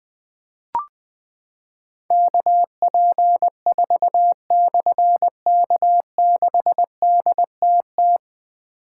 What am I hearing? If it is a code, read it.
Morse code, heard as a keyed tone, about 20 wpm, 700 Hz. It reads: KP4/K6DTT
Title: Difficult Call Signs 20wpm